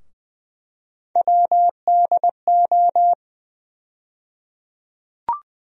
Morse code, heard as a keyed tone, about 20 wpm, 700 Hz. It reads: WDO
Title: USA Call Sign Postfixes - 3 Letters 20wpm